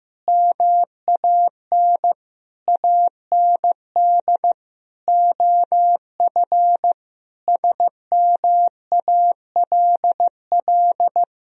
Morse code, heard as a keyed tone, about 16 wpm, 700 Hz. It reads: MAN AND OF SMALL
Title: Sentences from Top 1000 Words - Review Part 3 - 15wpm